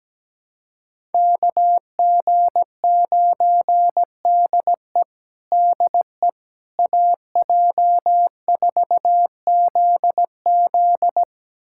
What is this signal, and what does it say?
Morse code, heard as a keyed tone, about 17 wpm, 700 Hz. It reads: KG9DE DE AJ4ZZ
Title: USA Callsign DE Callsign - 5 Characters - 17wpm